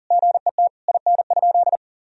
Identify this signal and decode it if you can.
Morse code, heard as a keyed tone, about 40 wpm, 690 Hz. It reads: GET IN?
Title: Sentences from Top 1000 Words - Review Part 4 - 40wpm